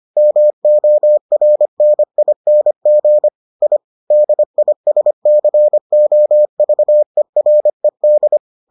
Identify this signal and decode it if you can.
Morse code, heard as a keyed tone, about 25 wpm, 600 Hz. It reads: MORNING I DISCOVERED